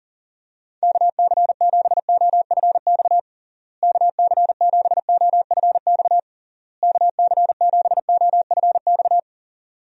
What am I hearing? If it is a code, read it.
Morse code, heard as a keyed tone, about 40 wpm, 700 Hz. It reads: KC7OFX KC7OFX KC7OFX
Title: USA Call Signs - 6 Characters - Repeated 3x - 40wpm